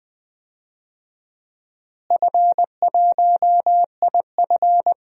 Morse code, heard as a keyed tone, about 20 wpm, 700 Hz. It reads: F1IF
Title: DXCC-100 Call Signs - 4 Characters - Repeated 3x - 20wpm